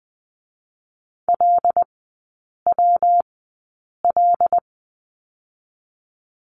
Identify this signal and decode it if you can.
Morse code, heard as a keyed tone, about 20 wpm, 700 Hz. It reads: L W L
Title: 114 Lesson 19e - Letter w - Three Characters 20wpm